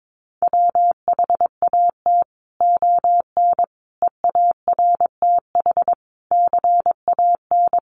Morse code, heard as a keyed tone, about 22 wpm, 700 Hz. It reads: WHAT ON EARTH CAN
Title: Sentences from Top 700 Words 22wpm